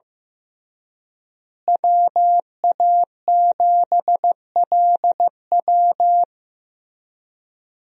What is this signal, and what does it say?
Morse code, heard as a keyed tone, about 15 wpm, 700 Hz. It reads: WA7LW
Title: USA Call Signs - 5 Characters - Repeated 3x - 15wpm